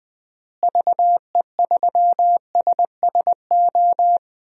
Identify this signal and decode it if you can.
Morse code, heard as a keyed tone, about 20 wpm, 700 Hz. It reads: VE3SSO
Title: DXCC-100 Call Signs - 6 Characters - Repeated 3x - 20wpm